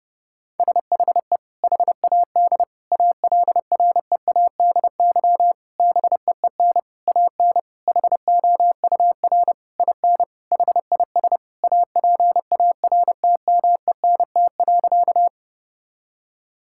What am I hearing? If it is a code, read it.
Morse code, heard as a keyed tone, about 32 wpm, 700 Hz. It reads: SHE HAD ALREADY BEEN AN HOUR IN HIS APARTMENT.
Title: Sentences from Top 1000 Words - Review Part 6 - 30wpm